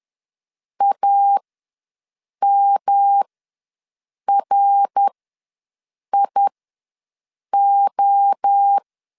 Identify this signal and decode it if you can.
Morse code, heard as a keyed tone, about 10 wpm, spaced as for 6 wpm, 800 Hz. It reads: AMRIO
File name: LEZIONI / 5 LBVC / 5 LBV ws 10wpm 3645